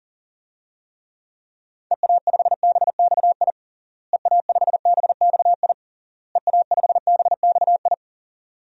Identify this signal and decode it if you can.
Morse code, heard as a keyed tone, about 40 wpm, 700 Hz. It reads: EA5BXI EA5BXI EA5BXI
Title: DXCC-100 Call Signs - 6 Characters - Repeated 3x - 40wpm